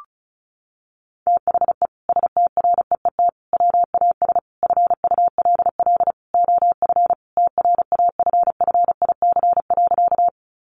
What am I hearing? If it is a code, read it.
Morse code, heard as a keyed tone, about 34 wpm, 700 Hz. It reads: THE STREET WAS FULL OF TRAFFIC.
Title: Sentences from Top 1300 Words 35wpm